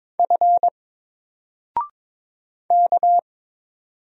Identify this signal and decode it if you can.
Morse code, heard as a keyed tone, about 22 wpm, 700 Hz. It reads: F K
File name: Single-Letter-Number-22wpm